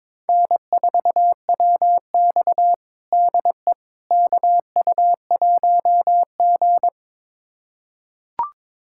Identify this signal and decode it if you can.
Morse code, heard as a keyed tone, about 22 wpm, 700 Hz. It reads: N4WX DE KU1G
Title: USA Callsign DE Callsign - 4 Characters - 22wpm